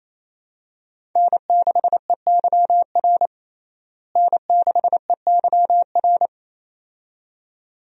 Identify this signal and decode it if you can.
Morse code, heard as a keyed tone, about 28 wpm, 700 Hz. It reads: N6EYR N6EYR
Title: USA Call Signs - 5 Characters - Repeated 2x - 28wpm